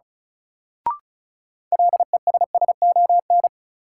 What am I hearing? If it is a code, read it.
Morse code, heard as a keyed tone, about 36 wpm, 700 Hz. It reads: LESSON
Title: Sentences from Top 1200 Words - New Word Drill Alt - 35wpm